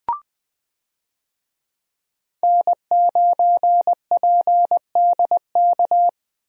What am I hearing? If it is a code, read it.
Morse code, heard as a keyed tone, about 20 wpm, 700 Hz. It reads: N9PDK
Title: USA Call Signs - 5 Characters - 20wpm